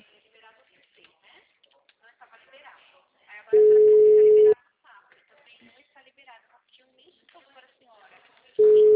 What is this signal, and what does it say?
Ringback tone (Europe, 425 Hz), with speech.